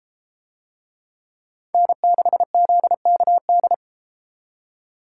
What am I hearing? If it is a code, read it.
Morse code, heard as a keyed tone, about 31 wpm, 700 Hz. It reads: N6ZKD